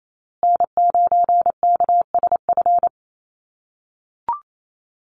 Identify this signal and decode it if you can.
Morse code, heard as a keyed tone, about 28 wpm, 700 Hz. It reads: N9KSF
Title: USA Call Signs Repeated 2x 28wpm